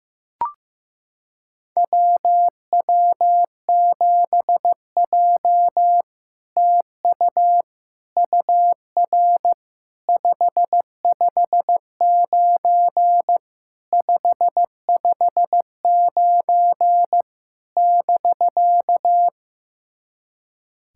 Morse code, heard as a keyed tone, about 15 wpm, 700 Hz. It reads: WW7J TU UR 559 559 <BK>
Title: SOTA QSO Practice 15wpm